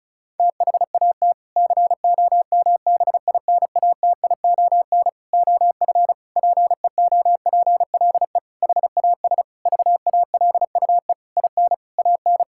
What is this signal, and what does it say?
Morse code, heard as a keyed tone, about 36 wpm, 700 Hz. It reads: THAT COMBINATION OF PEOPLE HAS VALUE IN AN
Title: Sentences from Top 1000 Words - Review Part 6 - 35wpm